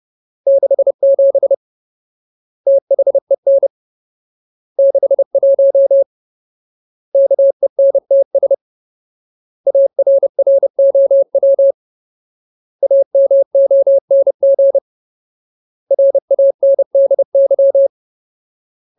Morse code, heard as a keyed tone, about 30 wpm, 550 Hz. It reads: 67 THEN 61 KENTS ARROW AMONG RANDY
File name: Ultimate Word List / 30WPM Ultimate